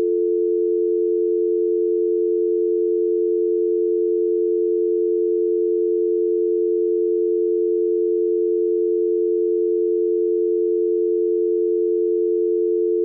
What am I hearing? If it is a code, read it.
Dial tone (North America), 350 + 440 Hz.